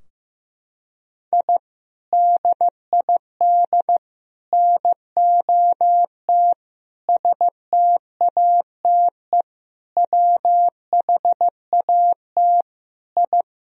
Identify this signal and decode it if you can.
Morse code, heard as a keyed tone, about 15 wpm, 700 Hz. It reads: I DID NOT STATE WHAT I
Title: Sentences from Top 1000 Words - Review Part 2 - 15wpm